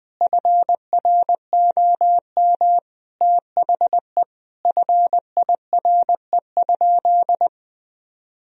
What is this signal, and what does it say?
Morse code, heard as a keyed tone, about 21 wpm, 700 Hz. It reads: FROM THE FIRE?
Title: Sentences from Top 1000 Words - Review Part 3 - 20wpm